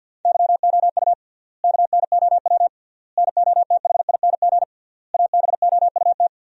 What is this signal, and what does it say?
Morse code, heard as a keyed tone, about 50 wpm, 700 Hz. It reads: YOU KNOW NOTHING ABOUT
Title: 3x Word-Spacing - Sentences from Top 500 Words - 50wpm